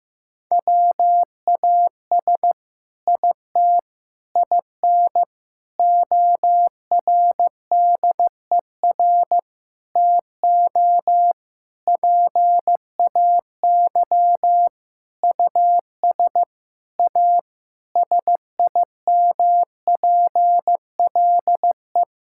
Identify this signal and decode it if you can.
Morse code, heard as a keyed tone, about 15 wpm, 700 Hz. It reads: WAS IT IN ORDER TO PAY US A SIMPLE
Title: Sentences from Top 1000 Words - Review Part 6 - 15wpm